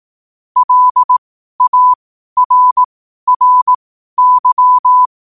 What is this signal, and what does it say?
Morse code, heard as a keyed tone, about 18 wpm, spaced as for 12 wpm, 1000 Hz. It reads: LARRY
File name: qotdfiles / 12wpm